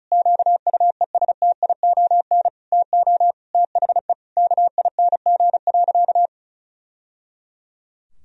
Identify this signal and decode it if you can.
Morse code, heard as a keyed tone, about 36 wpm, 700 Hz. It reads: QUESTION TO THE KING.
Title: Sentences from Top 1000 Words - Review Part 2 - 35wpm